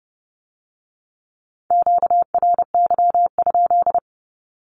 Morse code, heard as a keyed tone, about 30 wpm, 700 Hz. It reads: QRY?
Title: Q-Codes with Spoken Meaning - 30wpm